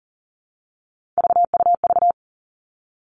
Morse code, heard as a keyed tone, about 38 wpm, 700 Hz. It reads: VUV
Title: Three Challenging Letter-Number Combinations 40wpm